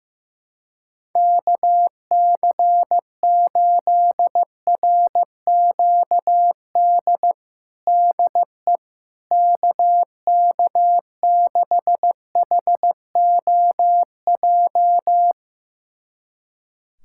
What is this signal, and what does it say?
Morse code, heard as a keyed tone, about 15 wpm, 700 Hz. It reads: KC8RQD DE KK6HOJ
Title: USA Callsign DE Callsign - 6 Characters - 15wpm